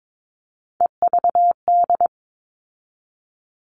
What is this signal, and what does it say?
Morse code, heard as a keyed tone, about 22 wpm, 700 Hz. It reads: EVD
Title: All Cut Numbers - 3-Digit 22wpm